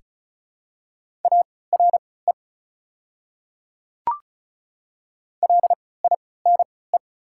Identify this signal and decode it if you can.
Morse code, heard as a keyed tone, about 34 wpm, spaced as for 20 wpm, 700 Hz. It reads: ARE LINE
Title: 3x Character Spacing - Top 200 Words - 35wpm